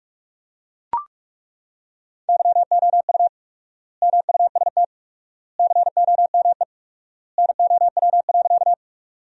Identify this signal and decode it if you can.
Morse code, heard as a keyed tone, about 43 wpm, 700 Hz. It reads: YOU MUST COME NOW.
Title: 4x Word-Spacing - Sentences from Top 200 Words - 45wpm